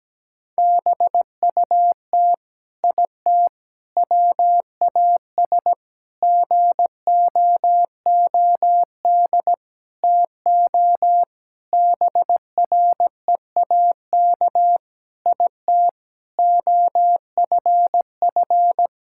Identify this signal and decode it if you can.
Morse code, heard as a keyed tone, about 17 wpm, 700 Hz. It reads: BUT IT WAS GOOD TO BREAK IT OFF
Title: Sentences from Top 1000 Words - Review Part 6 - 17wpm